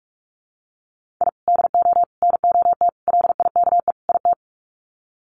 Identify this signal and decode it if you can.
Morse code, heard as a keyed tone, about 44 wpm, 700 Hz. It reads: I DO NOT LIKE IT
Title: Sets of 5 Words Encore 45wpm